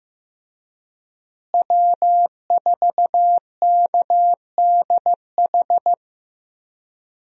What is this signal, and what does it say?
Morse code, heard as a keyed tone, about 15 wpm, 700 Hz. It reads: W4KDH
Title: USA Call Signs - 5 Characters - Repeated 2x - 15wpm